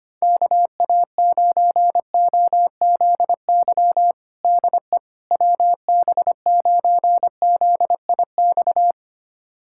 Morse code, heard as a keyed tone, about 25 wpm, 700 Hz. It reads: KA9OZY DE WB9ZIX
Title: USA Callsign DE Callsign - 6 Characters - 25wpm